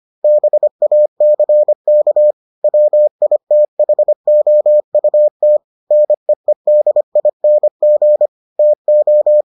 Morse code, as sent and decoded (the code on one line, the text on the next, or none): -... .- -.-. -.- / .-- .. - .... --- ..- - / -. . -.. .. -. --. / - ---
BACK WITHOUT NEEDING TO